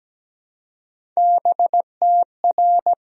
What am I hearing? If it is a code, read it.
Morse code, heard as a keyed tone, about 17 wpm, 700 Hz. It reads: BTR
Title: Sentences from Top 400 Words and Abbreviations 17wpm